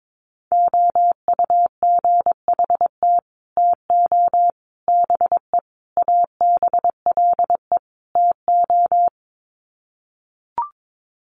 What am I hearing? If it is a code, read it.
Morse code, heard as a keyed tone, about 22 wpm, 700 Hz. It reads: OUGHT TO BE ABLE TO
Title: Sets of 5 Words Encore2 22wpm